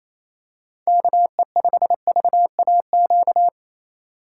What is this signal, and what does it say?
Morse code, heard as a keyed tone, about 28 wpm, 700 Hz. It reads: KE5VAQ
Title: USA Call Signs - 6 Characters - 4x Character Spacing - 28wpm